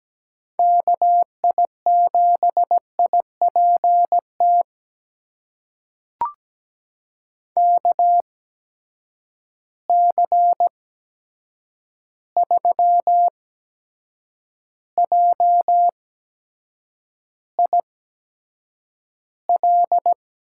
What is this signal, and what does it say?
Morse code, heard as a keyed tone, about 17 wpm, spaced as for 7 wpm, 700 Hz. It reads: KI7IPT KC3JIL
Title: USA Call Signs - 6 Characters - 8x Character Spacing - 17wpm